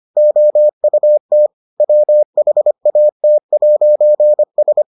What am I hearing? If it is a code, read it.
Morse code, heard as a keyed tone, about 25 wpm, 600 Hz. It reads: OUT WHAT'S